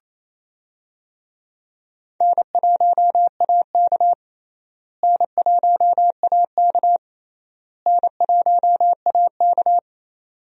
Morse code, heard as a keyed tone, about 28 wpm, 700 Hz. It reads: N1AK N1AK N1AK
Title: USA Call Signs - 4 Characters - Repeated 3x - 28wpm